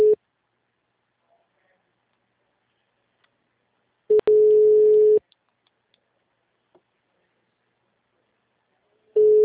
Ringback tone (Europe, 425 Hz).
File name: out- -2035- - - .36